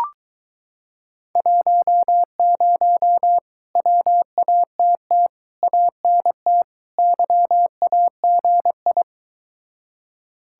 Morse code, heard as a keyed tone, about 22 wpm, 700 Hz. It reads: 10 WATT ANT YAGI